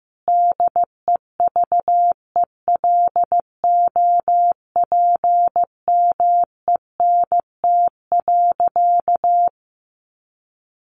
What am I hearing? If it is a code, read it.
Morse code, heard as a keyed tone, about 15 wpm, 700 Hz. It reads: DEVELOPMENT.
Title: Sentences from Top 1400 Words 15wpm